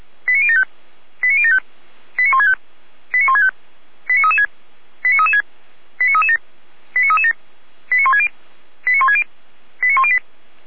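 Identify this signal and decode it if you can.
Five-tone selective call (ZVEI), heard as quick runs of five tones, tones of about 70 ms. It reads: 89075 89075 89165 89165 89336 89336 89337 89337 89160 89160 89108
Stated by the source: ZVEI five-tone selective call 89167, received by SDR at feuerwehr-orlamuende.org (fire brigade Orlamünde).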